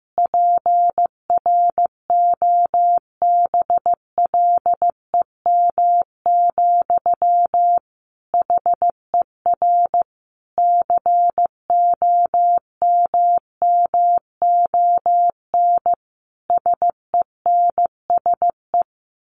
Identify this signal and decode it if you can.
Morse code, heard as a keyed tone, about 15 wpm, 700 Hz. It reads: PROBLEM, HER COMMON SENSE
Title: Sentences from Top 1200 Words 15wpm